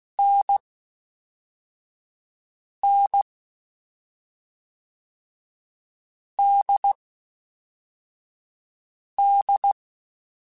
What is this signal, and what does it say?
Morse code, heard as a keyed tone, about 16 wpm, 800 Hz. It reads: N N D D
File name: Difficult Combos / Difficult Combos-mp3s / awj1 ndb6 gz7 @5